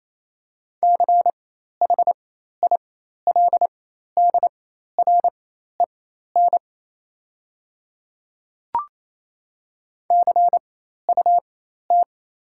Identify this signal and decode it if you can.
Morse code, heard as a keyed tone, about 28 wpm, spaced as for 13 wpm, 700 Hz. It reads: CHILDREN CUT
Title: 4x Character Spacing - Top 300 Words - 28wpm